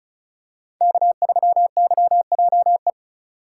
Morse code, heard as a keyed tone, about 36 wpm, 700 Hz. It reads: K3YJE
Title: USA Call Signs - 5 Characters - 3x Character Spacing - 35wpm